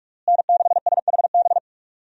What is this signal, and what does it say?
Morse code, heard as a keyed tone, about 42 wpm, 700 Hz. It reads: N6SHB
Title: USA Call Signs - 5 Characters - Repeated 2x - 45wpm